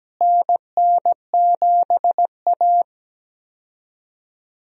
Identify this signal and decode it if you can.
Morse code, heard as a keyed tone, about 17 wpm, 700 Hz. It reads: NN7A
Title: USA Call Signs - 4 Characters - Repeated 2x - 17wpm